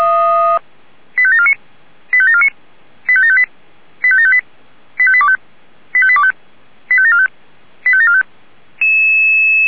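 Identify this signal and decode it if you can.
Five-tone selective call (ZVEI), heard as quick runs of five tones, tones of about 80 ms. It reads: 85749 85749 85758 85758 85825 85825 85745 85745
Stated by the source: ZVEI five-tone selective call received on SDR.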